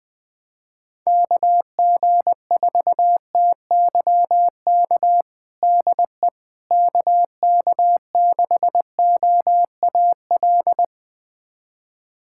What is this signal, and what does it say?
Morse code, heard as a keyed tone, about 20 wpm, 700 Hz. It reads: KG4TYK DE KK6OAL
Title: USA Callsign DE Callsign - 6 Characters - 20wpm